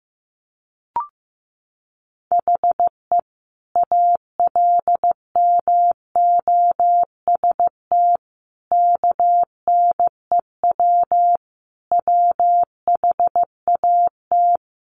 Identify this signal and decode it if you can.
Morse code, heard as a keyed tone, about 15 wpm, 700 Hz. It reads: HE ALMOST KNEW WHAT
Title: Sentences from Top 700 Words 15wpm